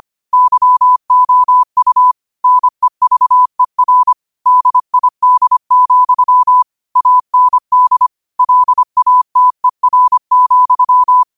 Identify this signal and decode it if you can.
Morse code, heard as a keyed tone, about 25 wpm, 1000 Hz. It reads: YOU NEVER DID, AND LATER,
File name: 25wpm